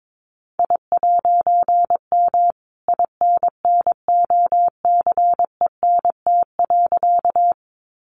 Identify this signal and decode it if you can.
Morse code, heard as a keyed tone, about 22 wpm, 700 Hz. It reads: I'M INNOCENT.
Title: A Sound Of Thunder 22wpm